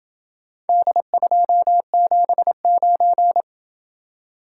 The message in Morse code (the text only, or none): D279